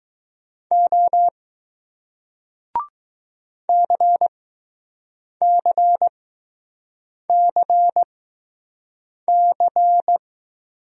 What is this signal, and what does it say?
Morse code, heard as a keyed tone, about 20 wpm, spaced as for 6 wpm, 700 Hz. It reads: O CCCC